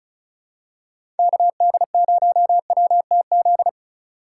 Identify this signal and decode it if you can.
Morse code, heard as a keyed tone, about 36 wpm, 700 Hz. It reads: KD0WTZ
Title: USA Call Signs - 6 Characters - 4x Character Spacing - 35wpm